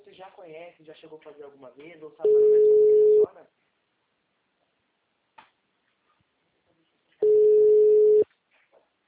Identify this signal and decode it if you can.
Ringback tone (Europe, 425 Hz), with speech.